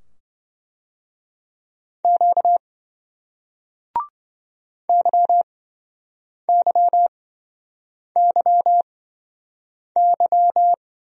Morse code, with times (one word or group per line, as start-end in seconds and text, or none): 2.0-2.6 Q
4.9-10.7 YYYY